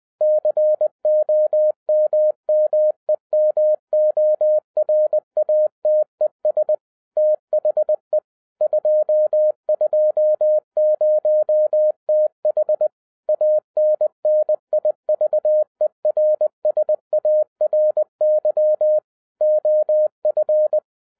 Morse code, heard as a keyed tone, about 20 wpm, 610 Hz. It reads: COMMEMORATES THE 220TH ANNIVERSARY OF